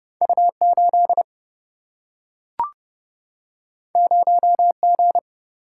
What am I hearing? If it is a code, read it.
Morse code, heard as a keyed tone, about 30 wpm, 700 Hz. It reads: U8 0G
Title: Single-Letter Single-Number 30wpm